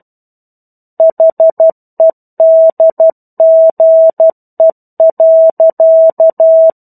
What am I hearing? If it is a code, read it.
Morse code, heard as a keyed tone, about 12 wpm, 650 Hz. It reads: HEDGE.
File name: 01-Chapter-01 12 WPM